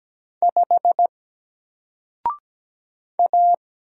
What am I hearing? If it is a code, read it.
Morse code, heard as a keyed tone, about 17 wpm, 700 Hz. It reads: 5 A
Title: Single Letter-Number Alt 17wpm